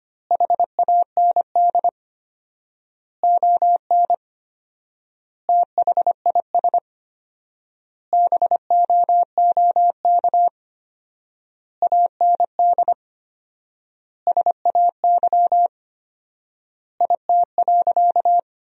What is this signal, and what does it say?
Morse code, heard as a keyed tone, about 25 wpm, 700 Hz. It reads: HAND ON THIS BOOK AND SAY IT.
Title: 4x Word-Spacing - Sentences from Top 400 Words - 25wpm